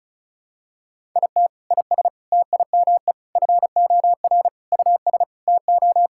Morse code, heard as a keyed tone, about 34 wpm, 700 Hz. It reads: IT IS TIME FOR US TO
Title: Sentences from Top 1000 Words - Review Part 6 - 35wpm